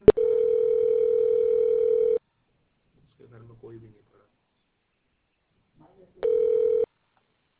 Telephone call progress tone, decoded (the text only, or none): ringback tone